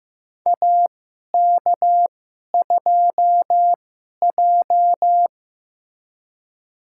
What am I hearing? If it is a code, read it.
Morse code, heard as a keyed tone, about 15 wpm, spaced as for 11 wpm, 700 Hz. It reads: AK2J
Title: USA Call Signs - 4 Characters - 2x Character Spacing - 15wpm